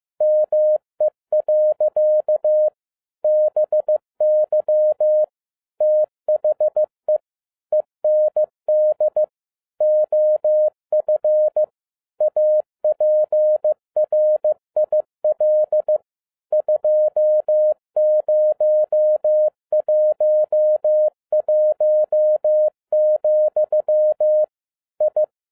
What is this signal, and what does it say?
Morse code, heard as a keyed tone, about 15 wpm, 610 Hz. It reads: ME. BY THE END OF APRIL 2011, I
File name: EXCITING QRP CW OPERATION 15wpm